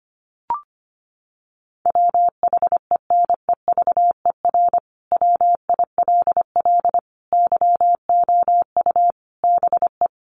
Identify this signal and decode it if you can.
Morse code, heard as a keyed tone, about 25 wpm, 700 Hz. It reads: WHENEVER WILL YOU BE
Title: Sentences from Top 1200 Words 25wpm